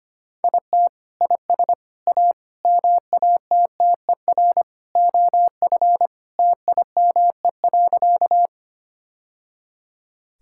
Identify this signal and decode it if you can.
Morse code, heard as a keyed tone, about 25 wpm, 700 Hz. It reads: IT IS A MATTER OF TIME.
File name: Sentences with 3-Word Phrases - 25wpm